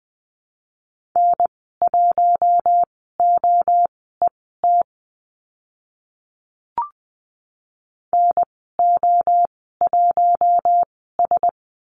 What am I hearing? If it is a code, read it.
Morse code, heard as a keyed tone, about 20 wpm, spaced as for 14 wpm, 700 Hz. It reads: N1OET NO1S
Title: 042 Lesson 8g - Number 1 - Call Signs 20wpm